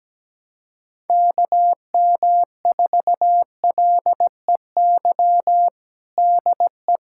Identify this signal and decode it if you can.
Morse code, heard as a keyed tone, about 17 wpm, 700 Hz. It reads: KM4LEY DE
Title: USA Callsign DE Callsign 17wpm